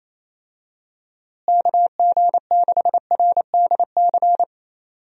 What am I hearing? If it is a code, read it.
Morse code, heard as a keyed tone, about 27 wpm, 700 Hz. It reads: KG6RDC
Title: USA Call Signs - 6 Characters - Repeated 2x - 28wpm